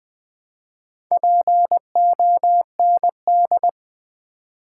Morse code, heard as a keyed tone, about 20 wpm, 700 Hz. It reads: POND
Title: Build A Word - 2 Letters At A Time - 4-Letter Words - 20wpm